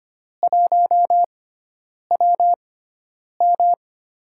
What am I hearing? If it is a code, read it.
Morse code, heard as a keyed tone, about 25 wpm, spaced as for 9 wpm, 700 Hz. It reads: 1WM
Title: USA Call Signs - 4 Characters - 6x Character Spacing - 25wpm